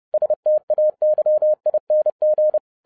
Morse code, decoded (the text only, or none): STAYING